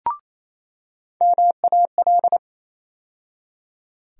Morse code, heard as a keyed tone, about 28 wpm, 700 Hz. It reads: MAL
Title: USA Call Sign Postfixes - 3 Letters 28wpm